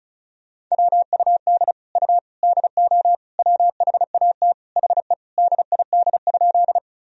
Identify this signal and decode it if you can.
Morse code, heard as a keyed tone, about 34 wpm, 700 Hz. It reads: WUD U DO WHAT HE DID?
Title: Sentences from Top 100 Words and Abbreviations 35wpm